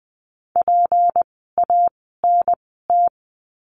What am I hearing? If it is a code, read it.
Morse code, heard as a keyed tone, about 20 wpm, spaced as for 14 wpm, 700 Hz. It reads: PANT
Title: 153 Lesson 25f - Letter p - Words 20wpm